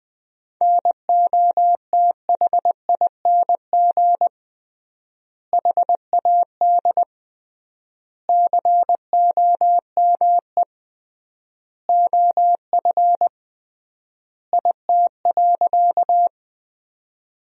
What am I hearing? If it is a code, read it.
Morse code, heard as a keyed tone, about 20 wpm, 700 Hz. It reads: NOTHING HAD COME OF IT.
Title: 3x Word-Spacing - Sentences from Top 500 Words - 20wpm